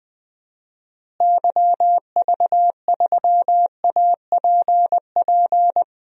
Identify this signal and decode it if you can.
Morse code, heard as a keyed tone, about 20 wpm, 700 Hz. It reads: YV3APP
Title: DXCC-100 Call Signs - 6 Characters - Repeated 2x - 20wpm